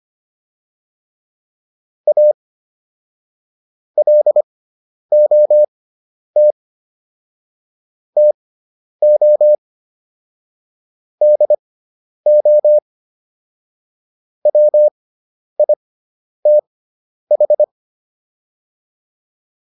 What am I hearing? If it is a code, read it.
Morse code, heard as a keyed tone, about 25 wpm, spaced as for 10 wpm, 600 Hz. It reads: A LOT TO DO WITH